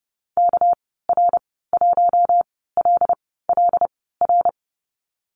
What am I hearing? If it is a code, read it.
Morse code, heard as a keyed tone, about 30 wpm, spaced as for 17 wpm, 700 Hz. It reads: KR1LLR
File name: Lesson 37g - Letter x - Call Signs 30wpm